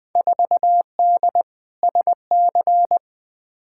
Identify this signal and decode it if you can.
Morse code, heard as a keyed tone, about 21 wpm, 700 Hz. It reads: 4D SC
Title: Field Day Exchanges Encore - 20wpm